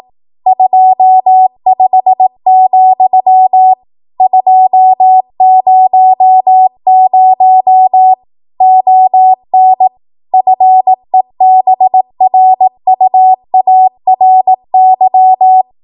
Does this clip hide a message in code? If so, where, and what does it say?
Yes: Morse code from 0.5 to 15.6 s, heard as a keyed tone, about 18 wpm, 750 Hz. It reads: 25, 200 ON FEBRUARY